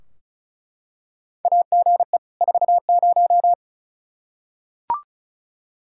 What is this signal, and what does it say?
Morse code, heard as a keyed tone, about 36 wpm, 700 Hz. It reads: AGE 40
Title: Practice QSO - Exchange 3 - More Info - 35wpm